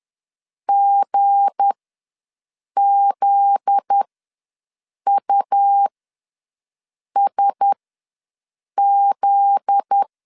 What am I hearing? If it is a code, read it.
Morse code, heard as a keyed tone, about 10 wpm, spaced as for 6 wpm, 800 Hz. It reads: GZUSZ
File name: LEZIONI / 7 GF9Z / 7 GF9Z ws 10wpm 3919